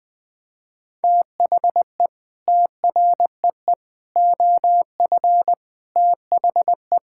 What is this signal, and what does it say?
Morse code, heard as a keyed tone, about 20 wpm, 700 Hz. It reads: THE TREE OF THE
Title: Sentences with 9-Word Phrases - 20wpm